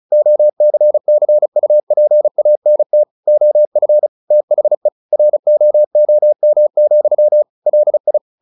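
Morse code, heard as a keyed tone, about 36 wpm, 600 Hz. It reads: OCCUPANT OF THE ROOM, LI
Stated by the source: Morse practice at 35 wpm from CW med Gnister, sending Ken Follett's novel Never in English.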